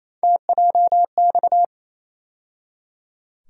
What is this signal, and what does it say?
Morse code, heard as a keyed tone, about 28 wpm, 700 Hz. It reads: TJX
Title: USA Call Sign Postfixes - 3 Letters 28wpm